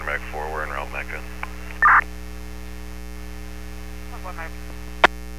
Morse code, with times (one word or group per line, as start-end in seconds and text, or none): none